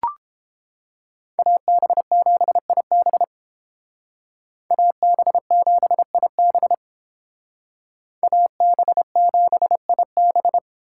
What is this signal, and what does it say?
Morse code, heard as a keyed tone, about 29 wpm, 700 Hz. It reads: AB7IB AB7IB AB7IB